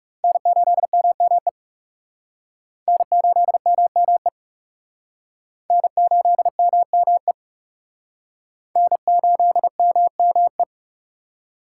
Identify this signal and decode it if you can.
Morse code, heard as a keyed tone, about 37 wpm, 700 Hz. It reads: N8MME N8MME N8MME N8MME
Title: Speed Racing - USA 1x3 Call Signs 30wpm